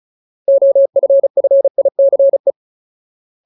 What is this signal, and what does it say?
Morse code, heard as a keyed tone, about 34 wpm, 550 Hz. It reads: OFFICE
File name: Ultimate Word List / 35WPM Ultimate